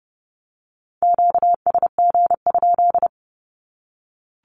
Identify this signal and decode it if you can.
Morse code, heard as a keyed tone, about 30 wpm, 700 Hz. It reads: QSG?
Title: Q-Codes 30wpm